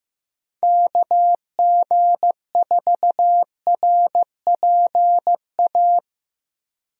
Morse code, heard as a keyed tone, about 15 wpm, 700 Hz. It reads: KG4RPA
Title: USA Call Signs - 6 Characters - Repeated 3x - 15wpm